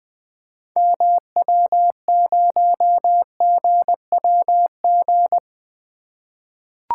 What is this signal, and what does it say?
Morse code, heard as a keyed tone, about 20 wpm, 700 Hz. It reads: MW0GWG
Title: 2023 POTA Activator Call Signs - 20wpm